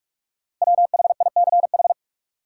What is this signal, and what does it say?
Morse code, heard as a keyed tone, about 46 wpm, 700 Hz. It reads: WHICH
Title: Sentences from Top 100 Words - New Word Drill - 45wpm